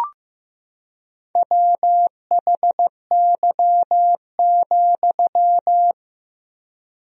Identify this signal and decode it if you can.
Morse code, heard as a keyed tone, about 15 wpm, 700 Hz. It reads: WHY,
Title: 3x Word-Spacing - Sentences from Top 200 Words - 15wpm